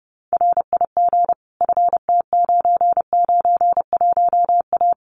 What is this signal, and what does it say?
Morse code, heard as a keyed tone, about 30 wpm, 700 Hz. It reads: RIG FT991A
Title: Common QSO Phrases Encore 30wpm